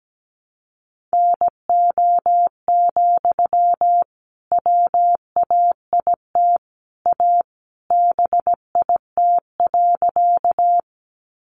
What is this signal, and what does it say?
Morse code, heard as a keyed tone, about 17 wpm, 700 Hz. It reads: NO, WAIT A BIT.
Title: Sentences from Top 900 Words 17wpm